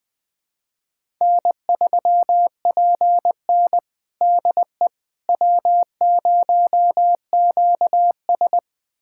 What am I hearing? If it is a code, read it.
Morse code, heard as a keyed tone, about 20 wpm, 700 Hz. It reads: N3PN DE W0QS
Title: USA Callsign DE Callsign - 4 Characters - 20wpm